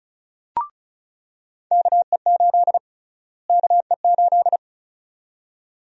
Morse code, heard as a keyed tone, about 36 wpm, 700 Hz. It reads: KE8 KE8
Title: USA Call Sign Prefixes - Repeated 2x - 35wpm